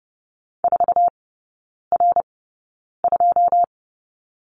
Morse code, heard as a keyed tone, about 30 wpm, spaced as for 9 wpm, 700 Hz. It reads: 4R2
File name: 081 Lesson 14e - Number 2 - Three Characters 30wpm